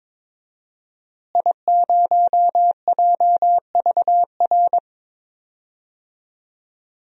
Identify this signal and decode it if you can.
Morse code, heard as a keyed tone, about 22 wpm, 700 Hz. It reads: I0JVR
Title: DXCC-100 Call Signs - 5 Characters - Repeated 3x - 22wpm